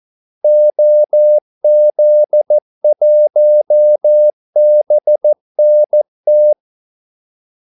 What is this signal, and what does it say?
Morse code, heard as a keyed tone, about 14 wpm, 600 Hz. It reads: OZ1BNT
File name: Callsigns-14wpm